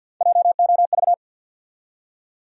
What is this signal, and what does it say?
Morse code, heard as a keyed tone, about 51 wpm, 700 Hz. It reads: JOV
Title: USA Call Sign Postfixes - 3 Letters 50wpm